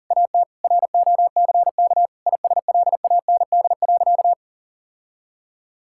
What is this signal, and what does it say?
Morse code, heard as a keyed tone, about 40 wpm, 700 Hz. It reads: AT ROCK ISLAND.
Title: Sentences from Top 1600 Words 40wpm